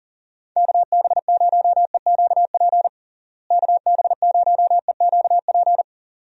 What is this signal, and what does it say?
Morse code, heard as a keyed tone, about 38 wpm, 700 Hz. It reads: KB0EQP KB0EQP
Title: USA Call Signs - 6 Characters - Repeated 2x - 40wpm